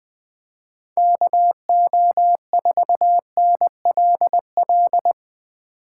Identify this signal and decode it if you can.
Morse code, heard as a keyed tone, about 20 wpm, 700 Hz. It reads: KO4NLL
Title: 2023 POTA Activator Call Signs - 20wpm